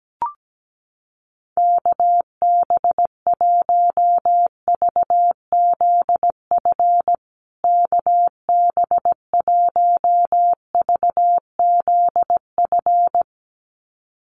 Morse code, heard as a keyed tone, about 17 wpm, 700 Hz. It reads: KB1VZF KB1VZF
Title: USA Call Signs Repeated 2x 17wpm